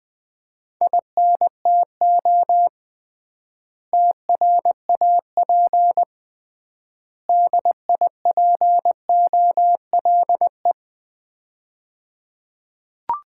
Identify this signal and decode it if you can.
Morse code, heard as a keyed tone, about 21 wpm, 700 Hz. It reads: INTO TRAP DIPOLE
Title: Common QSO Phrases - 3x Word-Spacing - 20wpm